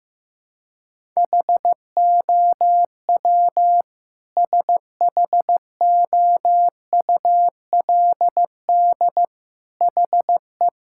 Morse code, heard as a keyed tone, about 15 wpm, 700 Hz. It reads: HOW SHOULD HE
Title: Sentences from Top 1000 Words - Review Part 6 - 15wpm